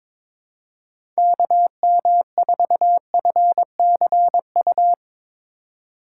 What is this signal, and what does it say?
Morse code, heard as a keyed tone, about 22 wpm, 700 Hz. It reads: KM4FCU
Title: USA Call Signs - 6 Characters - 3x Character Spacing - 22wpm